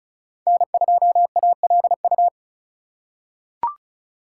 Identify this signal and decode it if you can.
Morse code, heard as a keyed tone, about 36 wpm, 700 Hz. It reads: N2ALU
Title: USA Call Signs - 3x Character Spacing - 35wpm